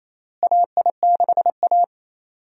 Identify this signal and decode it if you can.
Morse code, heard as a keyed tone, about 28 wpm, 700 Hz. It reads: AI6A
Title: USA Call Signs - 4 Characters - Repeated 2x - 28wpm